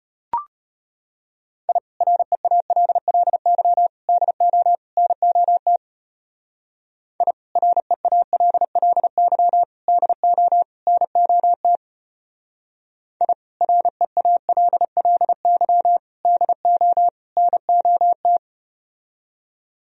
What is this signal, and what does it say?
Morse code, heard as a keyed tone, about 33 wpm, 700 Hz. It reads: I REALLY DO NOT I REALLY DO NOT I REALLY DO NOT